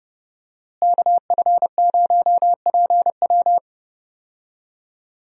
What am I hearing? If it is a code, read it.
Morse code, heard as a keyed tone, about 31 wpm, 700 Hz. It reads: KF0PW
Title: USA Call Signs - 5 Characters - Repeated 3x - 30wpm